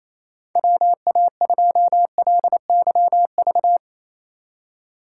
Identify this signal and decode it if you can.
Morse code, heard as a keyed tone, about 28 wpm, 700 Hz. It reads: WA2LYV
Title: USA Call Signs - 3x Character Spacing - 28wpm